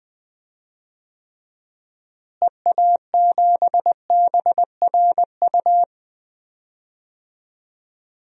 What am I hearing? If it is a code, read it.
Morse code, heard as a keyed tone, about 20 wpm, 700 Hz. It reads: EA7BRU
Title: DXCC-100 Call Signs - 6 Characters - Repeated 3x - 20wpm